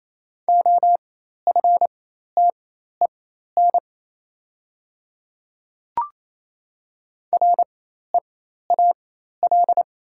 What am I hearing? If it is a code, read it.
Morse code, heard as a keyed tone, about 28 wpm, spaced as for 13 wpm, 700 Hz. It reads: OFTEN REAL
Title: 4x Character Spacing - Top 300 Words - 28wpm